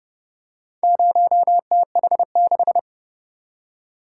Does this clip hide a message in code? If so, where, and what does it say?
Yes: Morse code from 0.8 to 2.8 s, heard as a keyed tone, about 30 wpm, 700 Hz. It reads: 0TH6